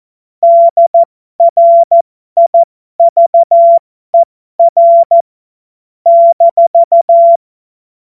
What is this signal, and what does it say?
Morse code, heard as a keyed tone, about 14 wpm, spaced as for 12 wpm, 680 Hz. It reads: DRIVER -